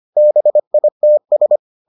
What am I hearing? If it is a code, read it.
Morse code, heard as a keyed tone, about 25 wpm, 600 Hz. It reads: BITS